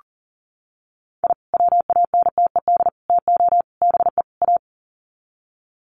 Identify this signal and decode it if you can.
Morse code, heard as a keyed tone, about 40 wpm, 700 Hz. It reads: I WANTED TO BE A
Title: Sets of 5 Words Encore2 40wpm